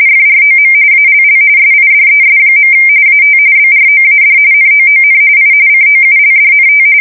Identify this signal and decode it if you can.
RTTY, about 45.45 baud. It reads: FROM ALL SIDES. AT BROAD NOON I COULDNT
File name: fsk170 / USS-New-Jersey